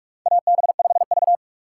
Morse code, heard as a keyed tone, about 46 wpm, 700 Hz. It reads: AB5V
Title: USA Call Signs - 4 Characters - 2x Character Spacing - 45wpm